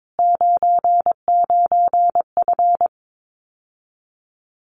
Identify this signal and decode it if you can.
Morse code, heard as a keyed tone, about 22 wpm, 700 Hz. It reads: 99F
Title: Common QSO Phrases Encore 22wpm